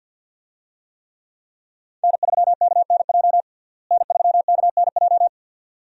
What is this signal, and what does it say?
Morse code, heard as a keyed tone, about 49 wpm, 700 Hz. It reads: N3KNJ N3KNJ
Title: USA Call Signs - 5 Characters - Repeated 2x - 50wpm